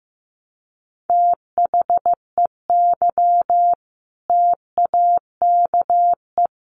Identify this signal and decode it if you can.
Morse code, heard as a keyed tone, about 15 wpm, 700 Hz. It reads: THEY TAKE
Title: Sentences from Top 900 Words 15wpm